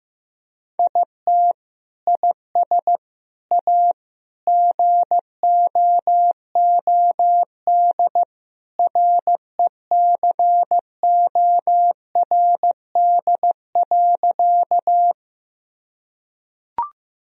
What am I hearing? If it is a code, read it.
Morse code, heard as a keyed tone, about 15 wpm, 700 Hz. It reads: IT IS A GOOD RECORD.
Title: Sentences from Top 1000 Words - Review Part 3 - 15wpm